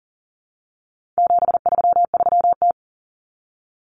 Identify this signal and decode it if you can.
Morse code, heard as a keyed tone, about 40 wpm, 700 Hz. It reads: 733T
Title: Common Cut Numbers - 4-Digit 40wpm